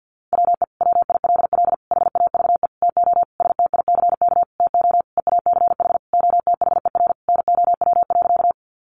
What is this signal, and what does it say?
Morse code, heard as a keyed tone, about 50 wpm, 700 Hz. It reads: WE WILL HAVE TO STICK TO EACH OTHER NOW.
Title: Sentences from Top 1200 Words 50wpm